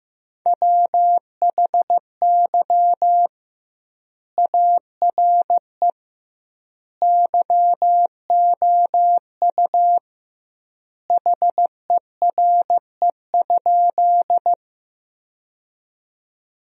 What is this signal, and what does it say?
Morse code, heard as a keyed tone, about 16 wpm, 700 Hz. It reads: WHY ARE YOU HERE?
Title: 2x Word-Spacing - Sentences from Top 200 Words - 15wpm